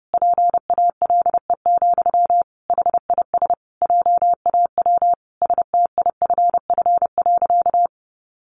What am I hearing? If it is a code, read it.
Morse code, heard as a keyed tone, about 28 wpm, 700 Hz. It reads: PALE, HIS JAW STIFF.
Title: A Sound Of Thunder 30wpm